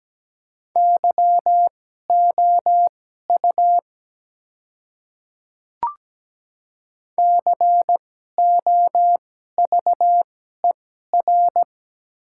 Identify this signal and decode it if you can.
Morse code, heard as a keyed tone, about 17 wpm, spaced as for 12 wpm, 700 Hz. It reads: YOU COVER
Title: 2x Character Spacing - Top 300 Words - 17wpm